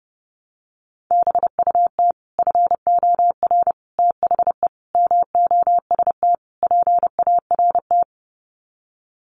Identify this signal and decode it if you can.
Morse code, heard as a keyed tone, about 30 wpm, 700 Hz. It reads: BUT FOR THE MOST PART
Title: Sets of 5 Words Encore 30wpm